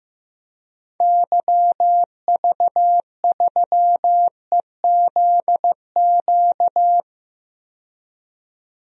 Morse code, heard as a keyed tone, about 15 wpm, 700 Hz. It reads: YV3EZQ
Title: DXCC-100 Call Signs - 6 Characters - 15wpm